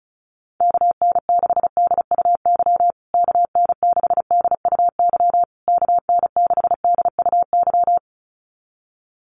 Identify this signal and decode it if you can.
Morse code, heard as a keyed tone, about 34 wpm, 700 Hz. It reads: KN6DUY KN6DUY KN6DUY
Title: USA Call Signs Repeated 3x 35wpm